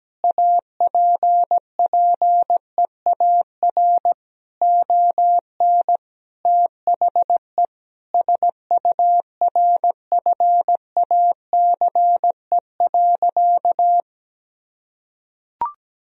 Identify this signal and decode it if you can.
Morse code, heard as a keyed tone, about 17 wpm, 700 Hz. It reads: APPEAR ON THE SURFACE.
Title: Sentences from Top 1000 Words - Review Part 3 - 17wpm